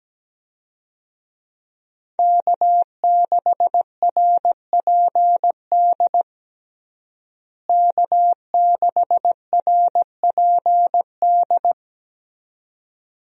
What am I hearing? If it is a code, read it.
Morse code, heard as a keyed tone, about 17 wpm, 700 Hz. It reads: K6RPD K6RPD
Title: USA Call Signs - 5 Characters - Repeated 2x - 17wpm